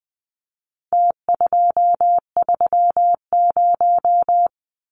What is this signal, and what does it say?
Morse code, heard as a keyed tone, about 20 wpm, 700 Hz. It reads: T230
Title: Numbers - Contest Style - 4-Digit 20wpm